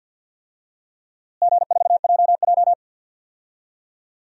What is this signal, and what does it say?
Morse code, heard as a keyed tone, about 47 wpm, 700 Hz. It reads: G4JJ
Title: DXCC-100 Call Signs - 4 Characters - Repeated 3x - 50wpm